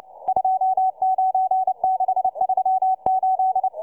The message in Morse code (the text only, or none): 29638